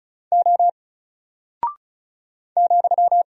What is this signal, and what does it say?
Morse code, heard as a keyed tone, about 36 wpm, 700 Hz. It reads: O ,
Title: Single Character (Letter:Number:Punctuation:Prosign) 35wpm